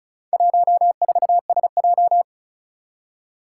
Morse code, heard as a keyed tone, about 36 wpm, 700 Hz. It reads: 14SJ
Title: Four Challenging Letter-Number Combinations 35wpm